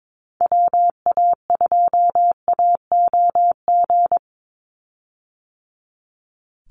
Morse code, heard as a keyed tone, about 22 wpm, 700 Hz. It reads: WA2AOG
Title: Call Signs - 22wpm